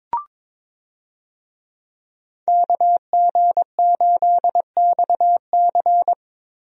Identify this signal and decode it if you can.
Morse code, heard as a keyed tone, about 22 wpm, 700 Hz. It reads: KG8XC
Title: USA Call Signs - 5 Characters - Repeated 3x - 22wpm